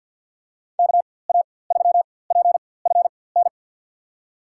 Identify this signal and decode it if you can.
Morse code, heard as a keyed tone, about 52 wpm, spaced as for 23 wpm, 700 Hz. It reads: KA3PFN
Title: USA Call Signs - 6 Characters - 4x Character Spacing - 50wpm